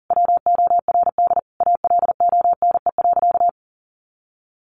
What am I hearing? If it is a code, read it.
Morse code, heard as a keyed tone, about 40 wpm, 700 Hz. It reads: WORD ALONE.
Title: Sentences from Top 1000 Words 40wpm